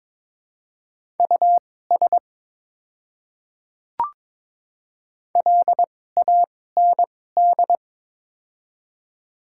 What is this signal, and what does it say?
Morse code, heard as a keyed tone, about 22 wpm, spaced as for 16 wpm, 700 Hz. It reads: US LAND
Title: 2x Character Spacing - Top 300 Words - 22wpm